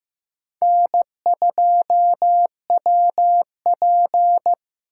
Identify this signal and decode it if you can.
Morse code, heard as a keyed tone, about 15 wpm, 700 Hz. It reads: N2WP
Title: USA Call Signs - 4x Character Spacing - 15wpm